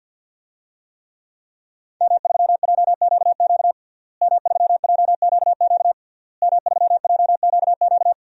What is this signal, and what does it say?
Morse code, heard as a keyed tone, about 52 wpm, 700 Hz. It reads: M3JQQ M3JQQ M3JQQ
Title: DXCC-100 Call Signs - 5 Characters - Repeated 3x - 50wpm